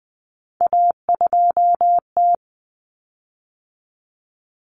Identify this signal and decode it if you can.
Morse code, heard as a keyed tone, about 20 wpm, 700 Hz. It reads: A2T
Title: Common Cut Numbers - 3-Digit 20wpm